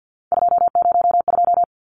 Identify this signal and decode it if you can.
Morse code, heard as a keyed tone, about 50 wpm, 700 Hz. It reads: 202
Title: Numbers - 3-Digit 50wpm